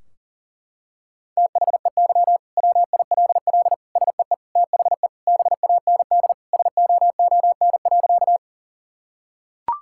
Morse code, heard as a keyed tone, about 40 wpm, 700 Hz. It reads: THEY WILL SEE THE BAND SOON.
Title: Sentences from Top 1600 Words 40wpm